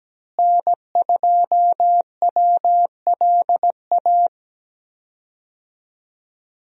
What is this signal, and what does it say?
Morse code, heard as a keyed tone, about 17 wpm, 700 Hz. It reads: N2WLA
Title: USA Call Signs - 5 Characters - Repeated 2x - 17wpm